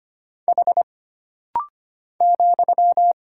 Morse code, heard as a keyed tone, about 25 wpm, 700 Hz. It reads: H ,
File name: Single Character (Letter:Number:Punctuation:Prosign) - Rapid-Fire - 25wpm